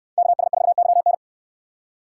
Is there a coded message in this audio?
Yes: Morse code from 0.2 to 1.1 s, heard as a keyed tone, about 65 wpm, 700 Hz. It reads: KS4YN